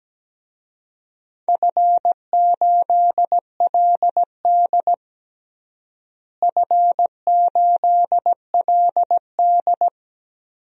Morse code, heard as a keyed tone, about 17 wpm, 700 Hz. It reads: F8LD F8LD
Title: DXCC-100 Call Signs - 4 Characters - Repeated 3x - 17wpm